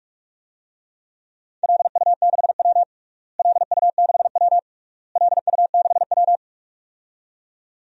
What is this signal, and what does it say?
Morse code, heard as a keyed tone, about 41 wpm, 700 Hz. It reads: LU6W LU6W LU6W
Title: DXCC-100 Call Signs - 4 Characters - Repeated 3x - 45wpm